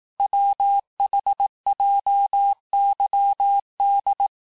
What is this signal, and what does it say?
Morse code, heard as a keyed tone, about 18 wpm, 800 Hz. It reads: WHJYD
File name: cours-du-lundi files / cours-1852-texte-chiffre-vitesse-17MPM